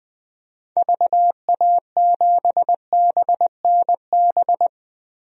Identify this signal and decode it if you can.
Morse code, heard as a keyed tone, about 20 wpm, 700 Hz. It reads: VA7BNB
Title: DXCC-100 Call Signs - 6 Characters - 20wpm